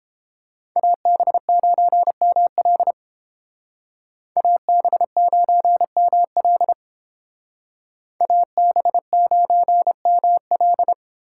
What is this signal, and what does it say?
Morse code, heard as a keyed tone, about 31 wpm, 700 Hz. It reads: AB9ML AB9ML AB9ML